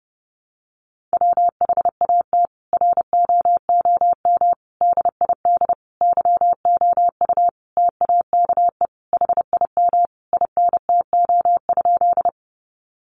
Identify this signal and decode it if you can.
Morse code, heard as a keyed tone, about 30 wpm, 700 Hz. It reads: WHAT ROOM DID YOU TAKE HIM INTO?
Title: Sentences from Top 600 Words 30wpm